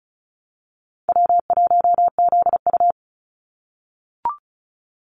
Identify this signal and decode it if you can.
Morse code, heard as a keyed tone, about 34 wpm, 700 Hz. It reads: W1ZU
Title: NA SOTA Activator Callsigns 35wpm